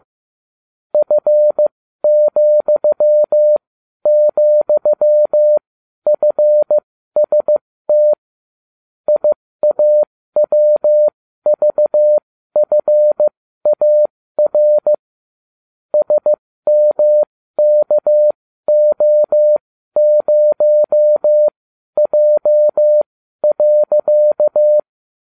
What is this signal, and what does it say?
Morse code, heard as a keyed tone, about 15 wpm, spaced as for 12 wpm, 600 Hz. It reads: F,,FST IAWVFAR SMKO0J.